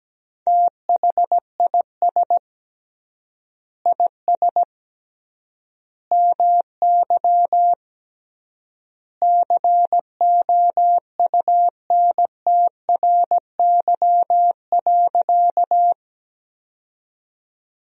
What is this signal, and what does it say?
Morse code, heard as a keyed tone, about 17 wpm, 700 Hz. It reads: THIS IS MY COUNTRY.
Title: 3x Word-Spacing - Sentences from Top 300 Words - 17wpm